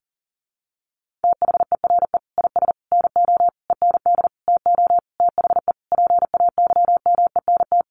Morse code, heard as a keyed tone, about 40 wpm, 700 Hz. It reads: THERE IS NO END TO THE PAYMENT
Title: Sentences from Top 1200 Words 40wpm